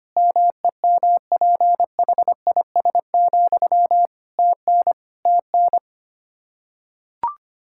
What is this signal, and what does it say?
Morse code, heard as a keyed tone, about 25 wpm, 700 Hz. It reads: MEMPHIS, TN TN
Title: Practice QSO - Exchange 1 - Beginning - 25wpm